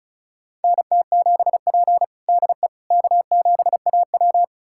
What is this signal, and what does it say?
Morse code, heard as a keyed tone, about 36 wpm, 700 Hz. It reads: NT7P DE K7AW
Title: USA Callsign DE Callsign - 4 Characters - 35wpm